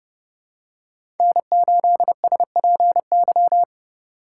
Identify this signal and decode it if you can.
Morse code, heard as a keyed tone, about 30 wpm, 700 Hz. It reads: N8SPY
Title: USA Call Signs - 5 Characters - 4x Character Spacing - 30wpm